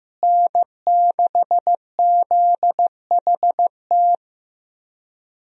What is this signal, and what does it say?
Morse code, heard as a keyed tone, about 15 wpm, 700 Hz. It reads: N6ZHT
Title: USA Call Signs - 5 Characters - Repeated 3x - 15wpm